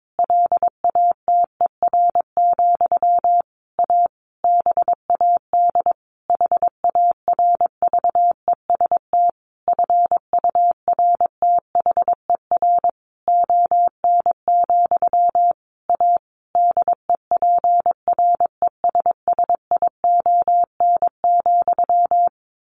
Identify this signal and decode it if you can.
Morse code, heard as a keyed tone, about 22 wpm, 700 Hz. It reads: LATER, A BAD HARVEST FURTHER ON, A DEPRESSION,
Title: A Sound Of Thunder 22wpm